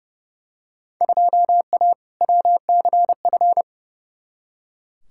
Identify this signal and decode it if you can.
Morse code, heard as a keyed tone, about 30 wpm, 700 Hz. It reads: 2A WCF
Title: Field Day Exchanges - 30wpm